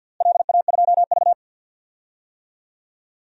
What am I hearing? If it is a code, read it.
Morse code, heard as a keyed tone, about 52 wpm, 700 Hz. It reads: LA2V
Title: DXCC-100 Call Signs - 4 Characters - Repeated 2x - 50wpm